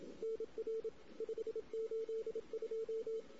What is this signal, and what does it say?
Morse code, heard as a keyed tone, about 27 wpm, 450 Hz. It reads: NR 582
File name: CW / 2008-ss-cw-k5tr-op-n5rz -16:00